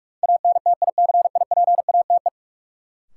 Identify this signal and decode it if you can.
Morse code, heard as a keyed tone, about 46 wpm, 700 Hz. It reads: ANTICIPATE
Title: Sentences from Top 1500 Words - New Word Drill Alt - 45wpm